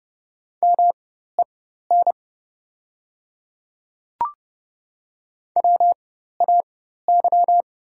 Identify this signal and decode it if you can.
Morse code, heard as a keyed tone, about 30 wpm, spaced as for 14 wpm, 690 Hz. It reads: MEN WAY
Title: 4x Character Spacing - Top 300 Words - 30wpm